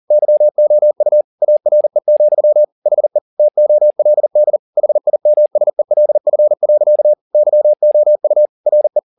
Morse code, heard as a keyed tone, about 40 wpm, 600 Hz. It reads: YOU ARE, HE TOLD HIMSELF. YOU RE
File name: the old man and the sea 2 40wpm